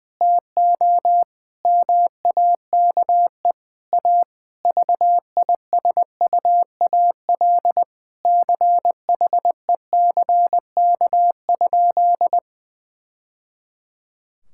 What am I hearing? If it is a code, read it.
Morse code, heard as a keyed tone, about 20 wpm, 700 Hz. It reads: TO MAKE A VISUAL CHECK?
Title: Sentences from Top 1000 Words - Review Part 6 - 20wpm